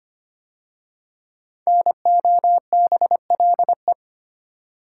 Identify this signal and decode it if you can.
Morse code, heard as a keyed tone, about 25 wpm, 700 Hz. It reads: NOBLE
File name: 5 Letter Words - First 3 Letters Given - 25wpm